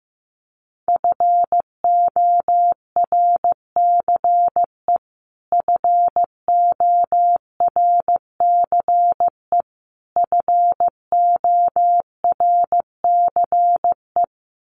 Morse code, heard as a keyed tone, about 15 wpm, 700 Hz. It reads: FORCE FORCE FORCE
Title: Top 500 Words Repeated 15wpm